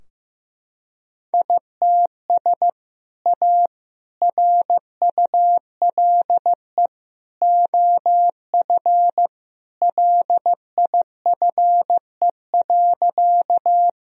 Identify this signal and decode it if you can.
Morse code, heard as a keyed tone, about 15 wpm, 700 Hz. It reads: ITS A RULE OF LIFE.
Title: Sentences from Top 1000 Words - Review Part 2 - 15wpm